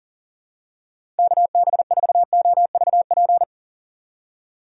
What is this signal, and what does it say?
Morse code, heard as a keyed tone, about 40 wpm, 700 Hz. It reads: KB4OVP
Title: USA Call Signs - 6 Characters - 2x Character Spacing - 40wpm